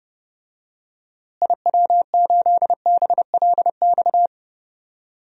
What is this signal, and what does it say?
Morse code, heard as a keyed tone, about 30 wpm, 700 Hz. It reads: IW8BLX
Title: DXCC-100 Call Signs - 6 Characters - Repeated 3x - 30wpm